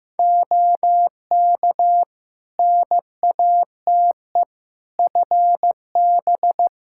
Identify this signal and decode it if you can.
Morse code, heard as a keyed tone, about 15 wpm, 700 Hz. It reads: OK NATE FB
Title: Practice QSO - Exchange 2 - Info - 15wpm